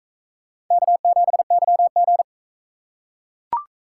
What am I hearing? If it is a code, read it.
Morse code, heard as a keyed tone, about 38 wpm, 700 Hz. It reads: K7YG